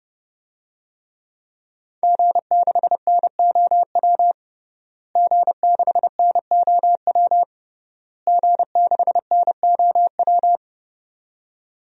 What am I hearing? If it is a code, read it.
Morse code, heard as a keyed tone, about 30 wpm, 700 Hz. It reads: G6NOW G6NOW G6NOW
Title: DXCC-100 Call Signs - 5 Characters - Repeated 3x - 30wpm